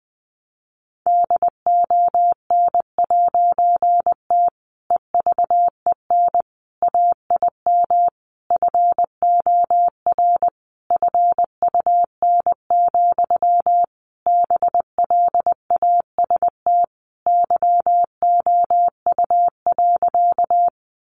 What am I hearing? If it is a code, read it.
Morse code, heard as a keyed tone, about 19 wpm, 700 Hz. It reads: DON'T EVEN AIM FOR FUN, BLAST YOU.
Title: A Sound Of Thunder 20wpm